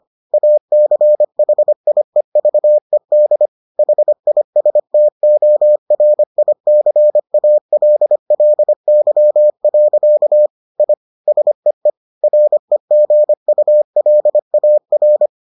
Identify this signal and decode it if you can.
Morse code, heard as a keyed tone, about 25 wpm, 600 Hz. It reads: ACHIEVED HISTORICALLY. I SEE REGULAR